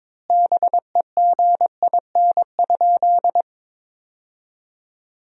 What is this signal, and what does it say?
Morse code, heard as a keyed tone, about 22 wpm, 700 Hz. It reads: BEGIN?
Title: 4x Word-Spacing - Sentences from Top 300 Words - 22wpm